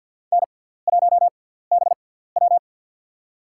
Morse code, heard as a keyed tone, about 53 wpm, spaced as for 17 wpm, 700 Hz. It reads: N1BW
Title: USA Call Signs - 4 Characters - 6x Character Spacing - 50wpm